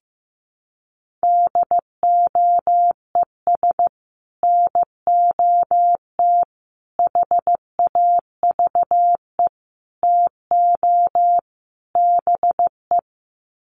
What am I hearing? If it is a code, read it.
Morse code, heard as a keyed tone, about 15 wpm, 700 Hz. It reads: DOES NOT HAVE TO BE
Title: Sets of 5 Words 15wpm